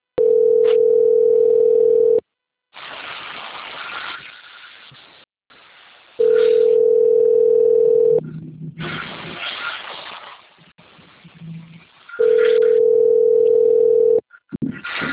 Ringback tone (North America), 440 + 480 Hz.